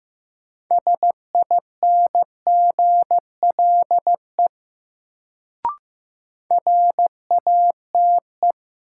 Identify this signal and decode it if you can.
Morse code, heard as a keyed tone, about 16 wpm, 700 Hz. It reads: SINGLE RATE
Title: Sentences from Top 700 Words - New Word Drill - 15wpm